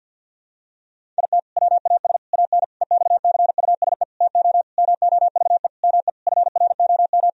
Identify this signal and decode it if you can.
Morse code, heard as a keyed tone, about 48 wpm, 700 Hz. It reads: IT WAS AN EXCUSE TO MOVE ME FROM
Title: Sentences from Top 1000 Words - Review Part 1 - 50wpm